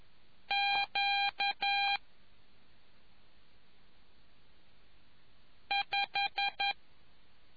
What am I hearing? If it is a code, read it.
Morse code, heard as a keyed tone, about 11 wpm, 800 Hz. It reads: Q 5